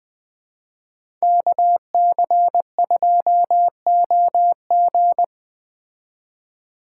Morse code, heard as a keyed tone, about 20 wpm, 700 Hz. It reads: KC2OG